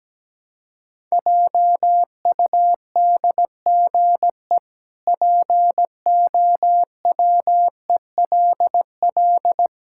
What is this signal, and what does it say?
Morse code, heard as a keyed tone, about 17 wpm, 700 Hz. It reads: JUDGE POWELL
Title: Pangram Challenge v2 - 17wpm